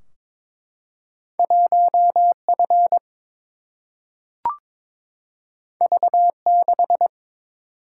Morse code, heard as a keyed tone, about 22 wpm, 700 Hz. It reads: 1 F V 6
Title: Single-Letter Single-Number 22wpm